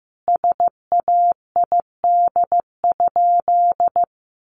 Morse code, heard as a keyed tone, about 15 wpm, 700 Hz. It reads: SAID?
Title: Sentences from Top 1500 Words 15wpm